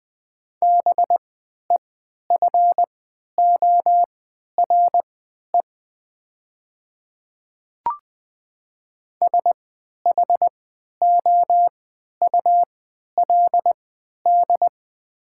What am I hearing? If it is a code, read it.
Morse code, heard as a keyed tone, about 20 wpm, spaced as for 11 wpm, 700 Hz. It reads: BEFORE SHOULD
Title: 3x Character Spacing - Top 300 Words - 20wpm